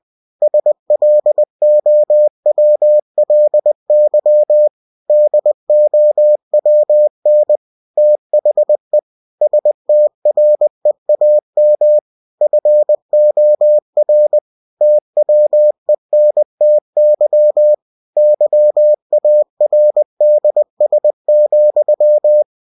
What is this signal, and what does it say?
Morse code, heard as a keyed tone, about 20 wpm, 600 Hz. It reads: SLOWLY DOWN THE STREAM FOR TWENTY YARDS,
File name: cw-books / winnie-the-pooh / winnie-20wpm-page-0044